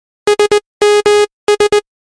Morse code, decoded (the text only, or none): SMS